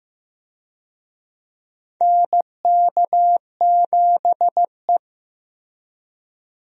Morse code, heard as a keyed tone, about 15 wpm, 700 Hz. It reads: NK7E
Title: USA Call Signs - 4 Characters - Repeated 2x - 15wpm